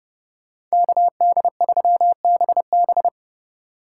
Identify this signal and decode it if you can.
Morse code, heard as a keyed tone, about 35 wpm, 700 Hz. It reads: KD3BB
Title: Speed Racing - USA 2x2 Call Signs 30wpm